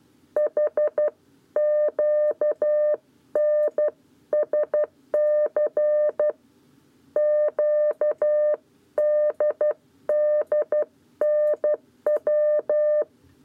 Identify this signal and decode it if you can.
Morse code, heard as a keyed tone, about 12 wpm, 600 Hz. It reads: HQNSC QDDNW
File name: 7b lezione cw 1